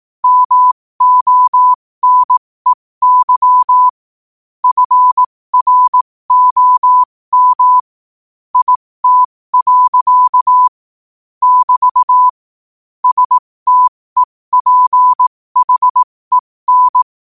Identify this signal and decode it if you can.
Morse code, heard as a keyed tone, about 18 wpm, spaced as for 15 wpm, 1000 Hz. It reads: MONEY FROM IT. <BT> STEPHEN
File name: qotdfiles / 15wpm